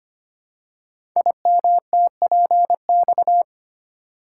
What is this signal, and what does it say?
Morse code, heard as a keyed tone, about 25 wpm, 700 Hz. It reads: IMTPX
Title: Code Groups - 5 Letters - 25wpm